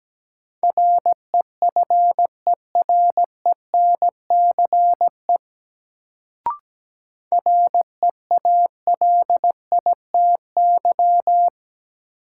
Morse code, heard as a keyed tone, about 17 wpm, 700 Hz. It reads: REFERENCE REALITY
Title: Sentences from Top 1200 Words - New Word Drill Alt - 17wpm